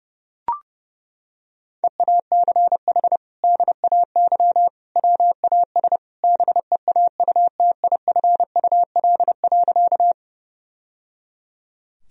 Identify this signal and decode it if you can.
Morse code, heard as a keyed tone, about 30 wpm, 700 Hz. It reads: EACH DAY WAS BEAUTIFUL.
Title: Sentences from Top 1000 Words - Review Part 1 - 30wpm